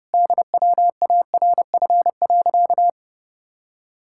Morse code, heard as a keyed tone, about 30 wpm, 700 Hz. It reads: DWARF.
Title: Pangram Challenge v2 - 30wpm